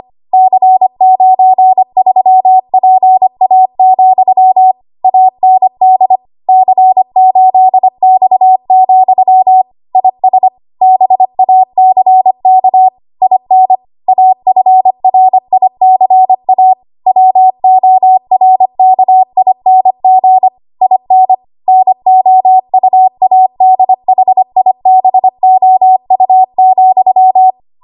Morse code, heard as a keyed tone, about 25 wpm, 750 Hz. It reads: C93PA, AND C8X, IS BACK IN AFRICA WORKING IN NOUADHIBOU,
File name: Archive / 25 WPM / 25WPM